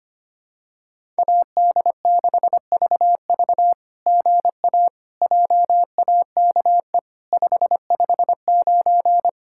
Morse code, heard as a keyed tone, about 25 wpm, 700 Hz. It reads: AD6VV GA JAKE 559
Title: SOTA QSO Practice 25wpm